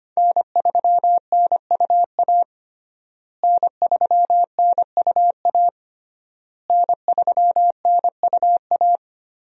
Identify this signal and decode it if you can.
Morse code, heard as a keyed tone, about 25 wpm, 700 Hz. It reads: N3NUA N3NUA N3NUA
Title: USA Call Signs - 5 Characters - Repeated 3x - 25wpm